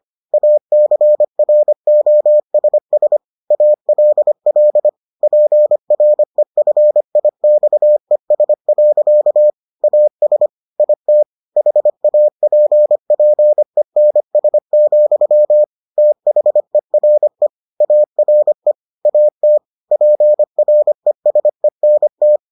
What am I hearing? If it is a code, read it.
Morse code, heard as a keyed tone, about 25 wpm, 600 Hz. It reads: ACROSS ALL PREFIXES. AS IT HAPPENS, THERE ARE AT PRESENT